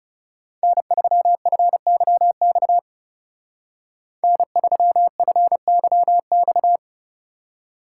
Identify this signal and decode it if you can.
Morse code, heard as a keyed tone, about 37 wpm, 700 Hz. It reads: N3FYX N3FYX
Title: Speed Racing - USA 1x3 Call Signs 30wpm